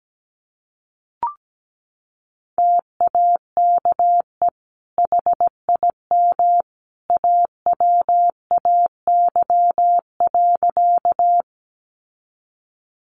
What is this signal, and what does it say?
Morse code, heard as a keyed tone, about 17 wpm, 700 Hz. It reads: TAKE HIM AWAY.
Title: Sentences from Top 600 Words 17wpm